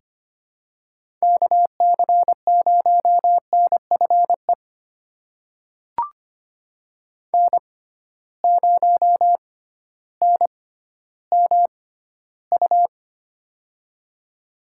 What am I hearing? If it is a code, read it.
Morse code, heard as a keyed tone, about 25 wpm, spaced as for 13 wpm, 700 Hz. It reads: KC0NFE N0NMU
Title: USA Call Signs - 6x Character Spacing - 25wpm